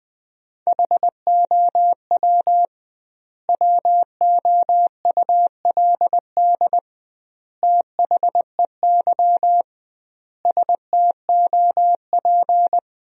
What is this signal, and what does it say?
Morse code, heard as a keyed tone, about 20 wpm, 700 Hz. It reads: HOW WOULD THEY STOP
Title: 2x Word-Spacing - Sentences from Top 300 Words - 20wpm